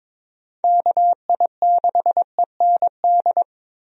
Morse code, heard as a keyed tone, about 22 wpm, 700 Hz. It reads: KI6END
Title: USA Call Signs - 6 Characters - Repeated 3x - 22wpm